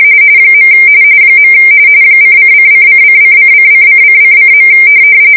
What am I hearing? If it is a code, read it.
RTTY, about 45.45 baud. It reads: . ::::::::::::::.